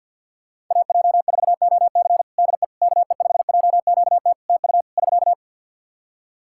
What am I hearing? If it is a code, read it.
Morse code, heard as a keyed tone, about 52 wpm, 700 Hz. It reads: AJ4OC DE KE5JQT TU <SK>
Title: K1USN SST QSO Practice 50wpm